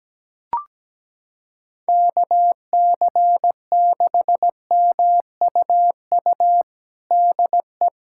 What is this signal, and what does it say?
Morse code, heard as a keyed tone, about 17 wpm, 700 Hz. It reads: KC6MUU DE
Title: USA Callsign DE Callsign - 6 Characters - 17wpm